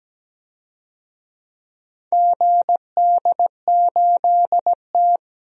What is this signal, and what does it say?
Morse code, heard as a keyed tone, about 17 wpm, 700 Hz. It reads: GD8T
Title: DXCC-100 Call Signs - 4 Characters - 17wpm